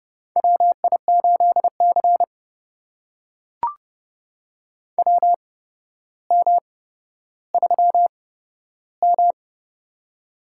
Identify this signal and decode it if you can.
Morse code, heard as a keyed tone, about 30 wpm, spaced as for 13 wpm, 700 Hz. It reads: WI8C WM3M
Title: USA Call Signs - 4 Characters - 8x Character Spacing - 30wpm